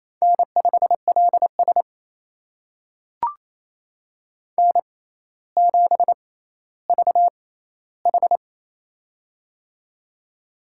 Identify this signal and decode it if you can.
Morse code, heard as a keyed tone, about 27 wpm, spaced as for 14 wpm, 700 Hz. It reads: N5LS N7VH
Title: USA Call Signs - 4 Characters - 6x Character Spacing - 28wpm